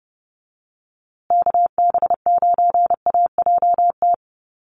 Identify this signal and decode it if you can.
Morse code, heard as a keyed tone, about 30 wpm, 700 Hz. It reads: KB9AJT
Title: USA Call Signs Repeated 3x 30wpm